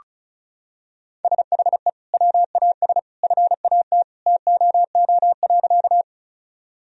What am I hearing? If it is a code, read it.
Morse code, heard as a keyed tone, about 36 wpm, 700 Hz. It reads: SHE WAS FAT TOO.
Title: Sentences from Top 1000 Words - Review Part 4 - 35wpm